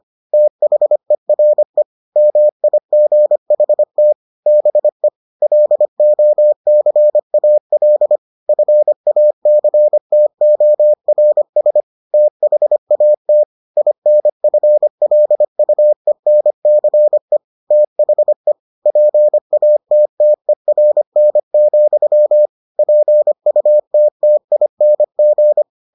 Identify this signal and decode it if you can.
Morse code, heard as a keyed tone, about 25 wpm, 600 Hz. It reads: THERE MIGHT BE LOCAL FACTORS THAT INFLUENCE THE PATTERN, PUTTING